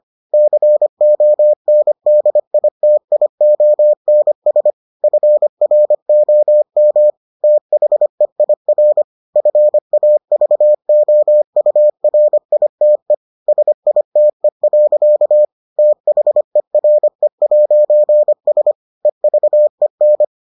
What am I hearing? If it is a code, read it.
Morse code, heard as a keyed tone, about 25 wpm, 600 Hz. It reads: CONDITIONS FROM THEIR FAVOURITE SITE. THERE'S EVEN